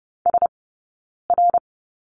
Morse code, heard as a keyed tone, about 30 wpm, 700 Hz. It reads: S R
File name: Lesson 20d - Number 3 - Two Characters 30wpm